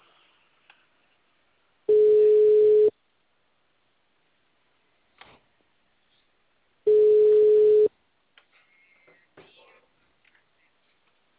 Ringback tone (Europe, 425 Hz).
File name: out- -2011- - - .151